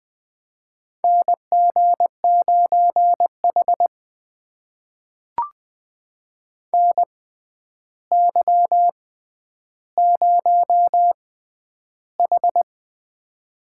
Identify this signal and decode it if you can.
Morse code, heard as a keyed tone, about 20 wpm, spaced as for 10 wpm, 700 Hz. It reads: NG9H NY0H
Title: USA Call Signs - 4 Characters - 6x Character Spacing - 20wpm